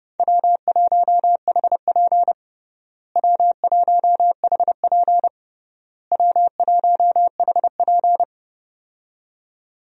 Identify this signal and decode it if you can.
Morse code, heard as a keyed tone, about 31 wpm, 700 Hz. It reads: W1HP W1HP W1HP
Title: USA Call Signs - 4 Characters - Repeated 3x - 30wpm